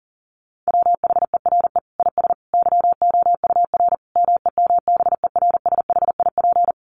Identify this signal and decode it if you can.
Morse code, heard as a keyed tone, about 37 wpm, 700 Hz. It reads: WHERE IS YOUR MEMBERSHIP
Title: Sentences from Top 1400 Words 40wpm